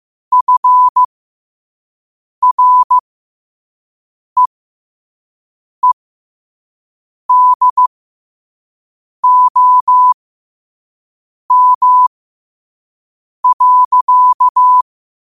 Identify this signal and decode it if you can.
Morse code, heard as a keyed tone, about 15 wpm, spaced as for 5 wpm, 1000 Hz. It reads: FREEDOM.
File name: qotdfiles / 5wpm